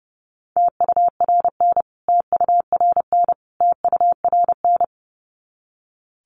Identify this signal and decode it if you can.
Morse code, heard as a keyed tone, about 30 wpm, 700 Hz. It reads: TURN TURN TURN
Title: Top 200 Words Repeated 30wpm